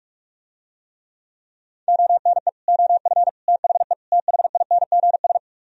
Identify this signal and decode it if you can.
Morse code, heard as a keyed tone, about 48 wpm, 700 Hz. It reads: ONE OF THE THINGS